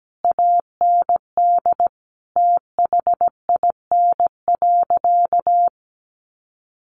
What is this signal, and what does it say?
Morse code, heard as a keyed tone, about 17 wpm, 700 Hz. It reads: AND THIN.
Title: Sentences from Top 1200 Words 17wpm